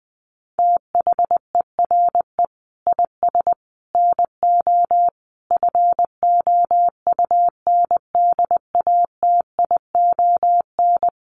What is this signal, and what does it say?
Morse code, heard as a keyed tone, about 20 wpm, 700 Hz. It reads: THERE IS NO FOUNDATION
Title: Sentences from Top 1200 Words 20wpm